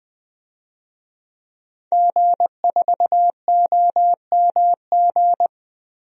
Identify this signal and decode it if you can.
Morse code, heard as a keyed tone, about 20 wpm, 700 Hz. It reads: G4OMG
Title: DXCC-100 Call Signs - 5 Characters - Repeated 3x - 20wpm